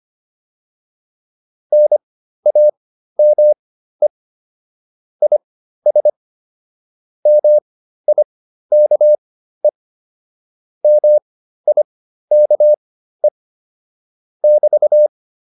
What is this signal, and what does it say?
Morse code, heard as a keyed tone, about 25 wpm, spaced as for 13 wpm, 600 Hz. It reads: NAME IS MIKE MIKE <BT>
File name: QSO310 13